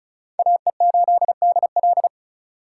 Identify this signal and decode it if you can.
Morse code, heard as a keyed tone, about 36 wpm, 700 Hz. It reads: AE8DL
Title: USA Call Signs - 5 Characters - 3x Character Spacing - 35wpm